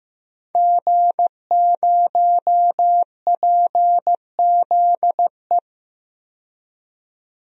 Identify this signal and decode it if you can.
Morse code, heard as a keyed tone, about 15 wpm, 700 Hz. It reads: G0PZE
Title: DXCC-100 Call Signs - 5 Characters - 15wpm